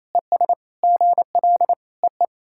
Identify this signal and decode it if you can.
Morse code, heard as a keyed tone, about 28 wpm, 700 Hz. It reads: ES GL EE
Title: SOTA QSO Practice 28wpm